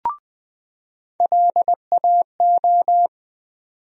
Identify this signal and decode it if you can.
Morse code, heard as a keyed tone, about 20 wpm, 700 Hz. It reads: LAO
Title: USA Call Sign Postfixes - 3 Letters 20wpm